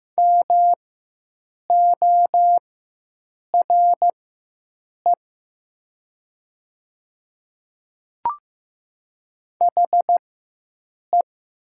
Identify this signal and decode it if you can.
Morse code, heard as a keyed tone, about 15 wpm, spaced as for 7 wpm, 700 Hz. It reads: MORE HE
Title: 4x Character Spacing - Top 200 Words - 15wpm